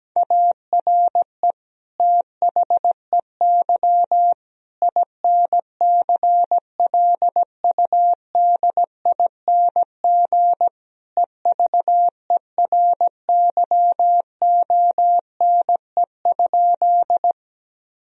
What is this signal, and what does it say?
Morse code, heard as a keyed tone, about 17 wpm, 700 Hz. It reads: ARE THEY INCLUDING EVERYONE?
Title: Sentences from Top 1000 Words - Review Part 4 - 17wpm